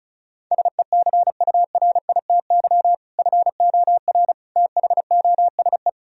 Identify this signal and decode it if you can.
Morse code, heard as a keyed tone, about 34 wpm, 700 Hz. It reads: SECURITY FOR THOSE
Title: Sentences from Top 1000 Words - Review Part 4 - 35wpm